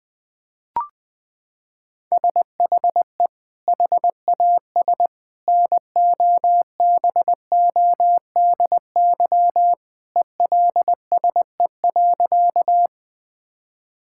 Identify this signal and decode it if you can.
Morse code, heard as a keyed tone, about 20 wpm, 700 Hz. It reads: SHE HAS NOBODY ELSE.
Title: Sentences from Top 1600 Words 20wpm